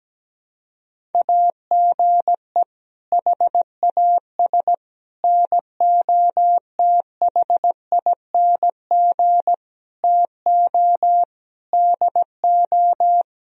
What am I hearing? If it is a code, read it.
Morse code, heard as a keyed tone, about 17 wpm, 700 Hz. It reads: AGE HAS NOTHING TO DO
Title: Sentences from Top 1000 Words - Review Part 3 - 17wpm